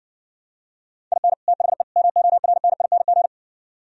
Morse code, heard as a keyed tone, about 62 wpm, 700 Hz. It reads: IN THE MORNING